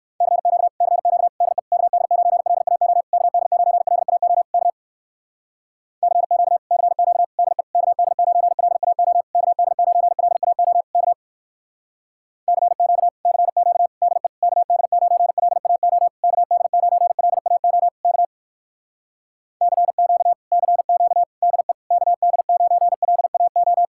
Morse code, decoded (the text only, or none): CQ CQ DE KD9LAO KD9LAO K CQ CQ DE KD9LAO KD9LAO K CQ CQ DE KD9LAO KD9LAO K CQ CQ DE KD9LAO